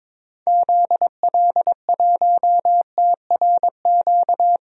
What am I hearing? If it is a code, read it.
Morse code, heard as a keyed tone, about 22 wpm, 700 Hz. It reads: ZL1TRQ
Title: DXCC-100 Call Signs - 6 Characters - Repeated 2x - 22wpm